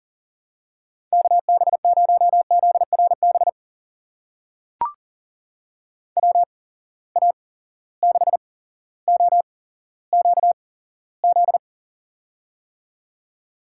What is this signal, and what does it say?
Morse code, heard as a keyed tone, about 37 wpm, spaced as for 17 wpm, 700 Hz. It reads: KB0ZRB WA6OQZ
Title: USA Call Signs - 6 Characters - 8x Character Spacing - 40wpm